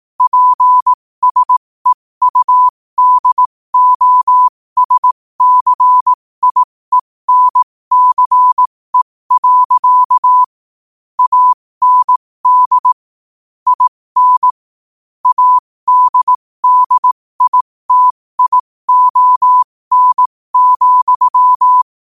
Morse code, heard as a keyed tone, about 18 wpm, spaced as for 15 wpm, 1000 Hz. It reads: PSEUDOSCIENCE. AND IN ADDITION,
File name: qotdfiles / 15wpm